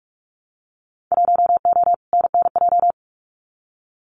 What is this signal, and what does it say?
Morse code, heard as a keyed tone, about 44 wpm, 700 Hz. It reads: 1O NNJ
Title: Winter Field Day Exchanges 45wpm